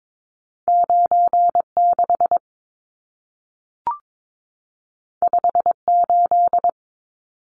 Morse code, heard as a keyed tone, about 22 wpm, 700 Hz. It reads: 96 58
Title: Numbers - 2-Digit 22wpm